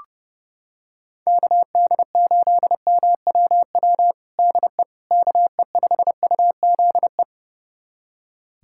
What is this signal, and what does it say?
Morse code, heard as a keyed tone, about 32 wpm, 700 Hz. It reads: KD8MWW DE KE5UZE
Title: USA Callsign DE Callsign - 6 Characters - 30wpm